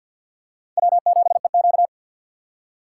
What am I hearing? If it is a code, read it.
Morse code, heard as a keyed tone, about 53 wpm, 700 Hz. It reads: W7EQ
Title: USA Call Signs - 4 Characters - 6x Character Spacing - 50wpm